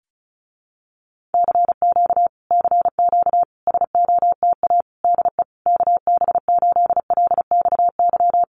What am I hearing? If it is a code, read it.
Morse code, heard as a keyed tone, about 34 wpm, 700 Hz. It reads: CQ CQ SOTA DE KB8LXY
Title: Common QSO Phrases Encore 35wpm